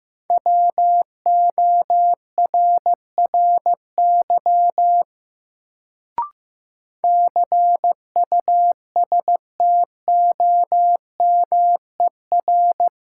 Morse code, heard as a keyed tone, about 15 wpm, 700 Hz. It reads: WORRY CUSTOMER
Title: Sentences from Top 1100 Words - New Word Drill Alt - 15wpm